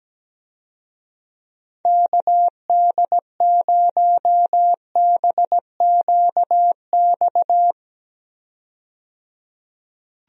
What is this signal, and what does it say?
Morse code, heard as a keyed tone, about 17 wpm, 700 Hz. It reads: KD0BQX
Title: USA Call Signs - 6 Characters - 17wpm